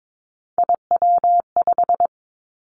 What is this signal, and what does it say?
Morse code, heard as a keyed tone, about 22 wpm, 700 Hz. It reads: IW5
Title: Three Challenging Letter-Number Combinations 22wpm